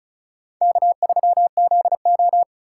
Morse code, heard as a keyed tone, about 36 wpm, 700 Hz. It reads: K3ZO
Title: USA Call Signs - 4 Characters - 3x Character Spacing - 35wpm